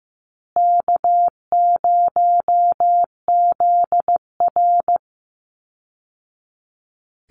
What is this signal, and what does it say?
Morse code, heard as a keyed tone, about 15 wpm, 700 Hz. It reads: K0ZR
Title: Easy Call Signs Repeated 2x - 15wpm